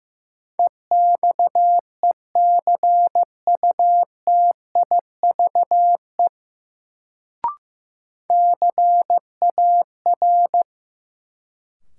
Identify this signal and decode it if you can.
Morse code, heard as a keyed tone, about 16 wpm, 700 Hz. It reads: EXECUTIVE CAR
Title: Sets of 5 Words Encore - New Word Drill - 15wpm